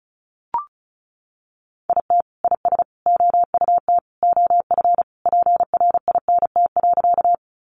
Morse code, heard as a keyed tone, about 34 wpm, 700 Hz. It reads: IT IS OUT OF PRINT.
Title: Sentences from Top 1400 Words 35wpm